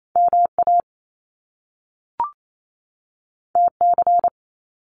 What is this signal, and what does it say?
Morse code, heard as a keyed tone, about 28 wpm, 700 Hz. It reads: MA TC
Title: Top Two Letter Combinations in Words 28wpm